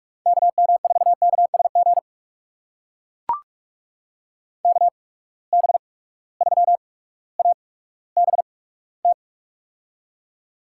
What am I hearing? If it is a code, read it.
Morse code, heard as a keyed tone, about 42 wpm, spaced as for 19 wpm, 700 Hz. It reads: KM4KSG KB3ABT
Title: USA Call Signs - 6 Characters - 8x Character Spacing - 45wpm